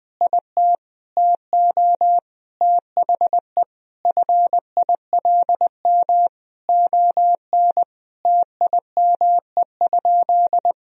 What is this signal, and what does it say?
Morse code, heard as a keyed tone, about 19 wpm, 700 Hz. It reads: IT TO THE FILM ON TIME?
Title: Sentences from Top 1000 Words - Review Part 3 - 20wpm